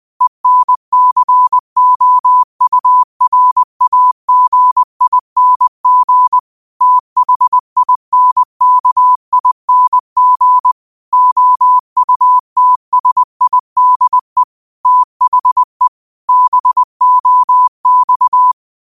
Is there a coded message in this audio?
Yes: Morse code from 0.2 to 18.5 s, heard as a keyed tone, about 20 wpm, 1000 Hz. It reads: ENCOURAGING THINKING OUTSIDE THE BOX